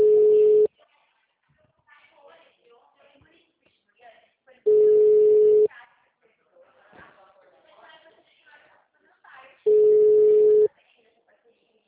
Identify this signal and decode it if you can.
Ringback tone (Europe, 425 Hz), with speech.